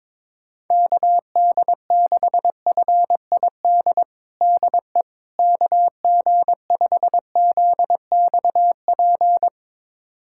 Morse code, heard as a keyed tone, about 22 wpm, 700 Hz. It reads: KD6FID DE KG5ZXP
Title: USA Callsign DE Callsign - 6 Characters - 22wpm